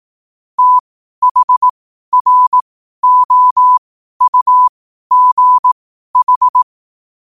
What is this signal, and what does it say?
Morse code, heard as a keyed tone, about 18 wpm, spaced as for 12 wpm, 1000 Hz. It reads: THROUGH
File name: qotdfiles / 12wpm